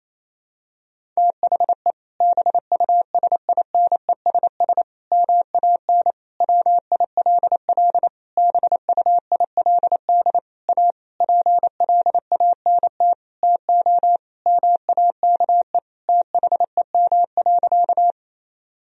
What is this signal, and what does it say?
Morse code, heard as a keyed tone, about 27 wpm, 700 Hz. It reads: THE BUSINESS MAN WILL BUILD A PLANT TO MAKE THEM.
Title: Sentences from Top 1000 Words - Review Part 1 - 28wpm